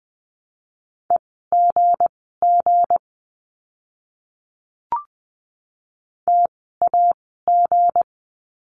Morse code, heard as a keyed tone, about 20 wpm, spaced as for 14 wpm, 700 Hz. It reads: EGG TAG
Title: 160 Lesson 26f - Letter g - Words 20wpm